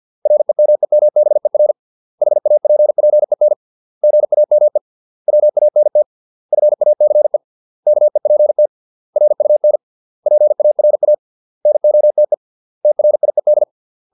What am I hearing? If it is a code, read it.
Morse code, heard as a keyed tone, about 50 wpm, 600 Hz. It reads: REMEMBER HAPPEN GAME WANT FACE KEPT RUN PARA NOTE TRIED
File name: 632 words / 632 Words 50WPM